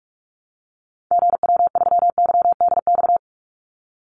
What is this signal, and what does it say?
Morse code, heard as a keyed tone, about 46 wpm, 700 Hz. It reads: GW3YDX
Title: Call Signs - 45wpm